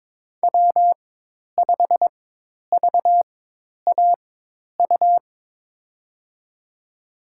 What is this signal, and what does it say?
Morse code, heard as a keyed tone, about 22 wpm, spaced as for 10 wpm, 700 Hz. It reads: W5VAU
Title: USA Call Signs - 5 Characters - 4x Character Spacing - 22wpm